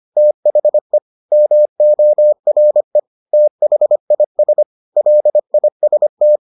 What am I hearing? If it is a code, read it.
Morse code, heard as a keyed tone, about 25 wpm, 600 Hz. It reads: THE MORE THIS LIST